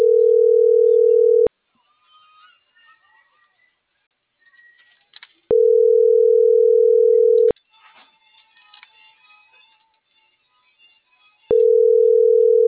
Ringback tone (North America), 440 + 480 Hz.